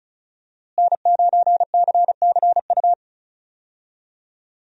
Morse code, heard as a keyed tone, about 36 wpm, 700 Hz. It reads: N9CCU
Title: USA Call Signs - 5 Characters - Repeated 3x - 35wpm